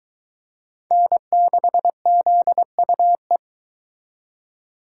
Morse code, heard as a keyed tone, about 22 wpm, 700 Hz. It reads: N6ZUE